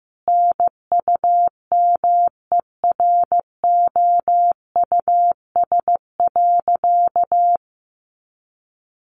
Morse code, heard as a keyed tone, about 15 wpm, 700 Hz. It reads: NUMEROUS.
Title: Sentences from Top 1400 Words 15wpm